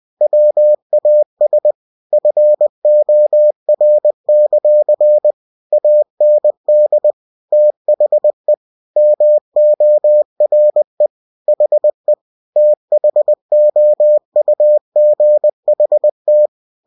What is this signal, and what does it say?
Morse code, heard as a keyed tone, about 20 wpm, 600 Hz. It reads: WAS FOR; AND THE MORE HE THOUGHT
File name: cw-books / winnie-the-pooh / winnie-20wpm-page-0103